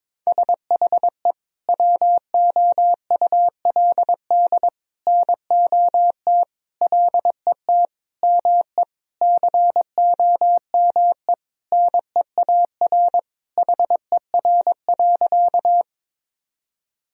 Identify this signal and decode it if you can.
Morse code, heard as a keyed tone, about 22 wpm, 700 Hz. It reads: SHE WOULD NOT LET ME COME NEAR HER.
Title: Sentences from Top 1000 Words - Review Part 1 - 22wpm